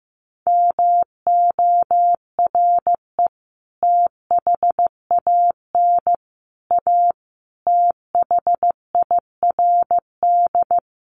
Morse code, heard as a keyed tone, about 15 wpm, 700 Hz. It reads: MORE THAN A THIRD
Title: Sets of 5 Words Encore2 15wpm